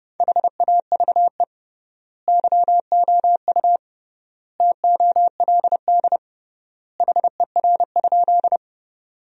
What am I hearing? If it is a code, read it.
Morse code, heard as a keyed tone, about 30 wpm, 700 Hz. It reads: HAVE YOU TOLD HER?
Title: 3x Word-Spacing - Sentences from Top 400 Words - 30wpm